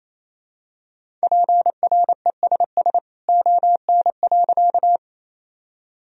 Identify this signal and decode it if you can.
Morse code, heard as a keyed tone, about 28 wpm, 700 Hz. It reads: PRESS ON.
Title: Sentences from Top 1000 Words - Review Part 1 - 28wpm